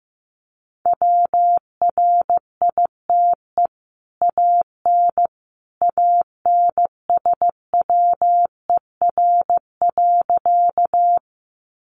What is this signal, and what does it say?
Morse code, heard as a keyed tone, about 15 wpm, 700 Hz. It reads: WRITE AN ANSWER.
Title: Sentences from Top 300 Words 15wpm